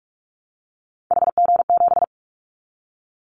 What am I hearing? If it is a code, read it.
Morse code, heard as a keyed tone, about 44 wpm, 700 Hz. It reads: HG7
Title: Callsign Prefixes for Top 25 DXCC - 45wpm